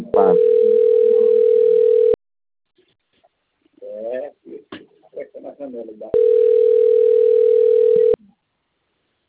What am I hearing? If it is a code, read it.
Ringback tone (North America), 440 + 480 Hz.